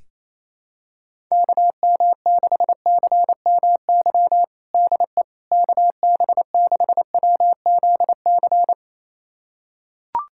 Morse code, heard as a keyed tone, about 27 wpm, 700 Hz. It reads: KM6CMY DE KB6WZC
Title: USA Callsign DE Callsign - 6 Characters - 28wpm